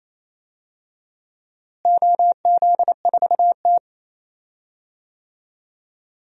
Morse code, heard as a keyed tone, about 28 wpm, 700 Hz. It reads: OZ4T
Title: DXCC-100 Call Signs - 4 Characters - Repeated 2x - 28wpm